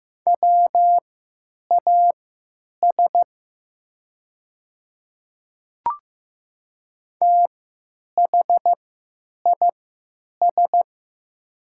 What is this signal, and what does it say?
Morse code, heard as a keyed tone, about 15 wpm, spaced as for 9 wpm, 700 Hz. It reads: WAS THIS
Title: 3x Character Spacing - Top 300 Words - 15wpm